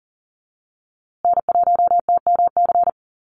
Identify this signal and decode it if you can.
Morse code, heard as a keyed tone, about 40 wpm, 700 Hz. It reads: N1TMC
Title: USA Call Signs Repeated 2x 40wpm